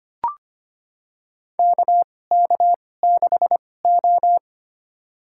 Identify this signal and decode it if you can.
Morse code, heard as a keyed tone, about 25 wpm, spaced as for 18 wpm, 700 Hz. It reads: KK6O
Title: USA Call Signs - 4 Characters - 2x Character Spacing - 25wpm